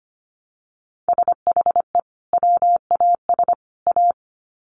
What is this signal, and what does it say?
Morse code, heard as a keyed tone, about 25 wpm, 700 Hz. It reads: SHE WAS A
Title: Sets of 3 Words 25wpm